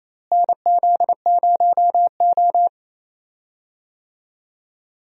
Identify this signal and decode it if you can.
Morse code, heard as a keyed tone, about 28 wpm, 700 Hz. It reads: NZ0O
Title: USA Call Signs - 4 Characters - Repeated 2x - 28wpm